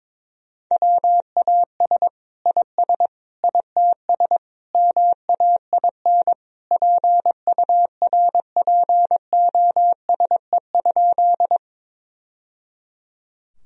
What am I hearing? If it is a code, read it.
Morse code, heard as a keyed tone, about 21 wpm, 700 Hz. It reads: WAS IS ITS MAIN PURPOSE?
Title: Sentences from Top 1000 Words - Review Part 2 - 22wpm